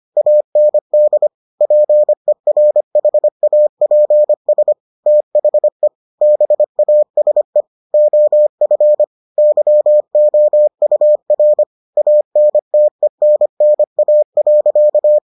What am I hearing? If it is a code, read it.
Morse code, heard as a keyed tone, about 25 wpm, 600 Hz. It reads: AND PERHAPS THE BASE OF YOUR ANTENNA.